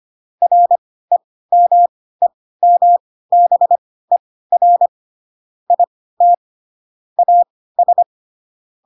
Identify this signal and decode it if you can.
Morse code, heard as a keyed tone, about 25 wpm, spaced as for 16 wpm, 700 Hz. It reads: REMEMBER IT AS
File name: Pinocchio / Pinocchio ch7 10 25 ik1hge